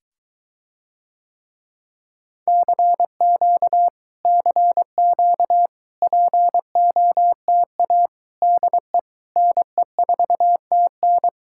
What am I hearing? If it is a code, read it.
Morse code, heard as a keyed tone, about 23 wpm, 700 Hz. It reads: CQ CQ POTA DE NE4TN